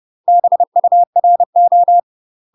Morse code, heard as a keyed tone, about 30 wpm, 700 Hz. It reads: BURO
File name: UK MASTER QSOs TRAINING / 30WPM / UK MASTER QSOs 0028